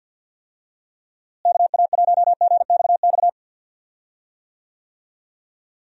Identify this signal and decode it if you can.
Morse code, heard as a keyed tone, about 52 wpm, 700 Hz. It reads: KA1GXX
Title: USA Call Signs - 6 Characters - Repeated 2x - 50wpm